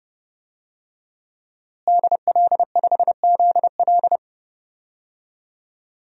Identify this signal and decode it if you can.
Morse code, heard as a keyed tone, about 30 wpm, 700 Hz. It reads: DL5ZL
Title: DXCC-100 Call Signs - 5 Characters - Repeated 3x - 30wpm